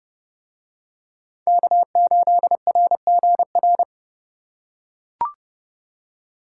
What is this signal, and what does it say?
Morse code, heard as a keyed tone, about 30 wpm, 700 Hz. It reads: K8RGR